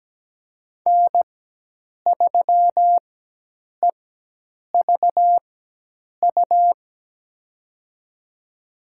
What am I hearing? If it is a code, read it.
Morse code, heard as a keyed tone, about 17 wpm, spaced as for 8 wpm, 700 Hz. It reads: N3EVU
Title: USA Call Signs - 5 Characters - 4x Character Spacing - 17wpm